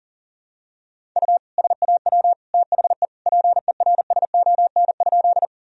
Morse code, heard as a keyed tone, about 37 wpm, 700 Hz. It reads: U SAW THE PERSON?
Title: Sentences from Top 400 Words and Abbreviations 40wpm